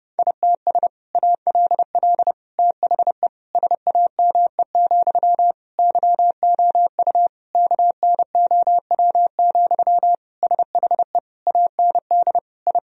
Morse code, heard as a keyed tone, about 31 wpm, 700 Hz. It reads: ITS ALL THE SAME, YOU KNOW, SHE AND I
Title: Sentences from Top 1000 Words - Review Part 3 - 30wpm